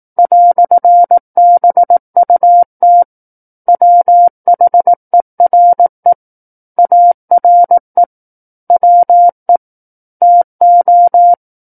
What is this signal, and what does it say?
Morse code, heard as a keyed tone, about 18 wpm, 700 Hz. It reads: "BUT WHERE ARE WE TO
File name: 18 WPM